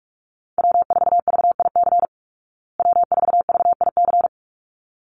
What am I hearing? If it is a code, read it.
Morse code, heard as a keyed tone, about 44 wpm, 700 Hz. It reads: W4VIC W4VIC
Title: Call Signs Repeated 3x - 45wpm